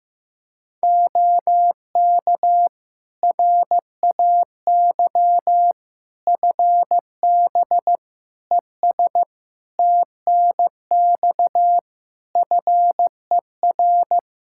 Morse code, heard as a keyed tone, about 15 wpm, 700 Hz. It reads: OK RAY FB ES TNX FER
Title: Practice QSO - Exchange 2 - Info - 15wpm